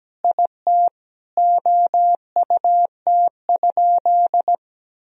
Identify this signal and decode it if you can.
Morse code, heard as a keyed tone, about 17 wpm, 700 Hz. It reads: IT OUT?
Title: Sentences from Top 1000 Words - Review Part 2 - 17wpm